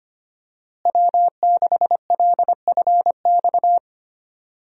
Morse code, heard as a keyed tone, about 25 wpm, 700 Hz. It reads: W6LFX
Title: USA Call Signs - 5 Characters - Repeated 2x - 25wpm